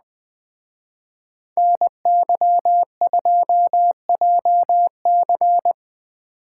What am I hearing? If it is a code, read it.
Morse code, heard as a keyed tone, about 21 wpm, 700 Hz. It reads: NY2JC
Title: Speed Racing - USA Call Signs 17wpm